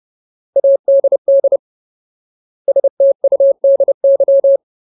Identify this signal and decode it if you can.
Morse code, heard as a keyed tone, about 30 wpm, 550 Hz. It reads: ADD STUDY
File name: Ultimate Word List / 30WPM Ultimate